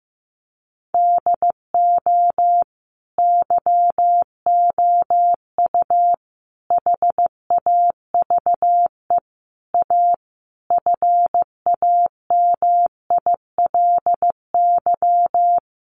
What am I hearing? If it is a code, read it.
Morse code, heard as a keyed tone, about 15 wpm, 700 Hz. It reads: DO YOU HAVE A FAMILY
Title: Sentences from Top 1200 Words 15wpm